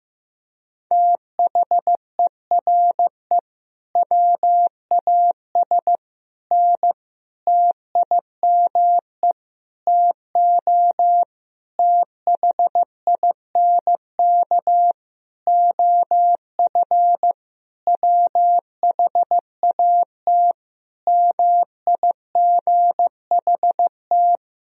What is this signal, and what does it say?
Morse code, heard as a keyed tone, about 16 wpm, 700 Hz. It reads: THERE WAS N TIME TO THINK OF WHAT MIGHT
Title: Sentences from Top 400 Words and Abbreviations 15wpm